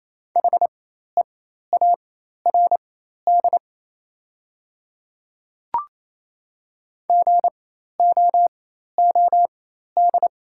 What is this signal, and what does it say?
Morse code, heard as a keyed tone, about 28 wpm, spaced as for 13 wpm, 700 Hz. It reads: HEARD GOOD
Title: 4x Character Spacing - Top 500 Words - 28wpm